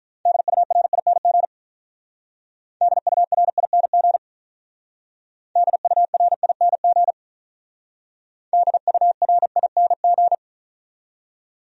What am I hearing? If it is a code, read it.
Morse code, heard as a keyed tone, about 43 wpm, 700 Hz. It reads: DURING DURING DURING DURING
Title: Speed-Racing Top 500 Words 35wpm